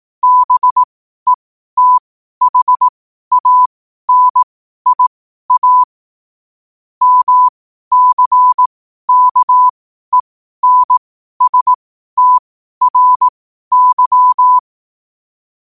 Morse code, heard as a keyed tone, about 18 wpm, spaced as for 12 wpm, 1000 Hz. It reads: BETHANIA MCKENSTRY
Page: Quotes for Wed, 13 Aug 2025 in Morse Code at 12 words per minute.